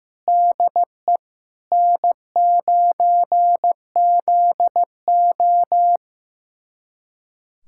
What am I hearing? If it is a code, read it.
Morse code, heard as a keyed tone, about 15 wpm, 700 Hz. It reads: DE N9ZO
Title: USA Callsign DE Callsign - 4 Characters - 15wpm